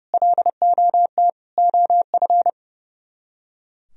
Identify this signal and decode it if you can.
Morse code, heard as a keyed tone, about 36 wpm, 700 Hz. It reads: LOT OF
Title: Speed-Racing - Sets of 4 Words 30wpm